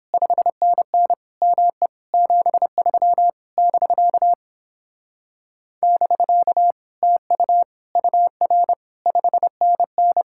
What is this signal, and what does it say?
Morse code, heard as a keyed tone, about 29 wpm, 700 Hz. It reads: 5NN ME 73 <BK> <BK> TU UR 5NN